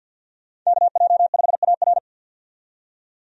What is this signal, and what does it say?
Morse code, heard as a keyed tone, about 49 wpm, 700 Hz. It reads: KJ5AR
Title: USA Call Signs - 5 Characters - 50wpm